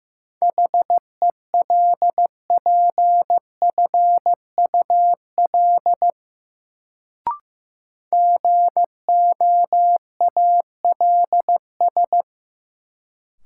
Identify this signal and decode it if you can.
Morse code, heard as a keyed tone, about 15 wpm, 700 Hz. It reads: HELPFUL GOALS
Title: Sentences from Top 1000 Words - Review Part 6 - New Word Drill - 15wpm